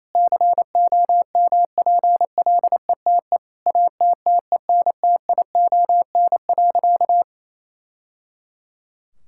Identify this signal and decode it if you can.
Morse code, heard as a keyed tone, about 28 wpm, 700 Hz. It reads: COMPLETE ATTENTION.
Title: Sentences from Top 1000 Words - Review Part 6 - 28wpm